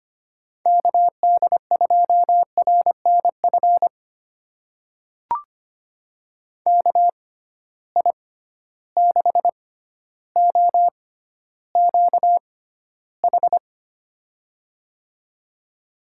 Morse code, heard as a keyed tone, about 25 wpm, spaced as for 13 wpm, 700 Hz. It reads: KD2RNF KI6OQH
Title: USA Call Signs - 6 Characters - 6x Character Spacing - 25wpm